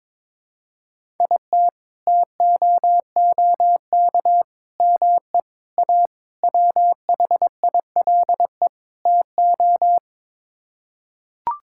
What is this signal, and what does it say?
Morse code, heard as a keyed tone, about 21 wpm, 700 Hz. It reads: IT TOOK ME A WHILE TO
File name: practice / 22wpm / Sets of 6 Words - 22wpm